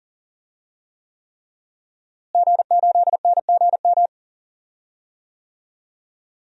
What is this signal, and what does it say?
Morse code, heard as a keyed tone, about 38 wpm, 700 Hz. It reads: G8NGM
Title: DXCC-100 Call Signs - 5 Characters - Repeated 3x - 40wpm